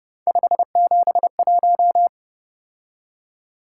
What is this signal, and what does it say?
Morse code, heard as a keyed tone, about 30 wpm, 700 Hz. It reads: 571